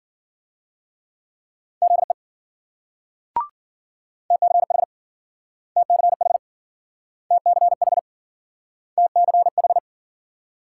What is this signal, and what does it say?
Morse code, heard as a keyed tone, about 52 wpm, 700 Hz. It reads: ZE TCH TCH TCH TCH